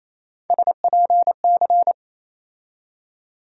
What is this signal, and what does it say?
Morse code, heard as a keyed tone, about 28 wpm, 700 Hz. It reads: SPC
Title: USA Call Sign Postfixes - 3 Letters 28wpm